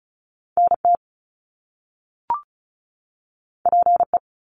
Morse code, heard as a keyed tone, about 36 wpm, 700 Hz. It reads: NT PE
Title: Canadian Provinces and Territories 35wpm